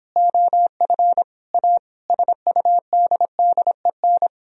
Morse code, heard as a keyed tone, about 29 wpm, 700 Hz. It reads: OF A SUDDEN